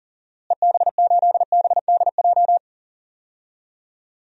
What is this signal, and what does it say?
Morse code, heard as a keyed tone, about 37 wpm, 700 Hz. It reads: EB8BDJ
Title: DXCC-100 Call Signs - 6 Characters - Repeated 3x - 40wpm